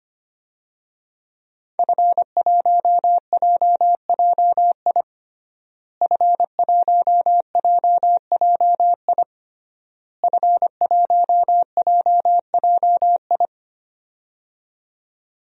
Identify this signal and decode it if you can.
Morse code, heard as a keyed tone, about 25 wpm, 700 Hz. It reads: F1JJI F1JJI F1JJI
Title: DXCC-100 Call Signs - 5 Characters - Repeated 3x - 25wpm